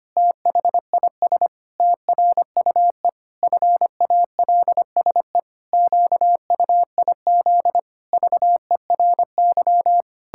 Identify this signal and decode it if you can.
Morse code, heard as a keyed tone, about 25 wpm, 700 Hz. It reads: THIS TRUE FALSE QUIZ VERY
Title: Pangram Challenge v2 - 25wpm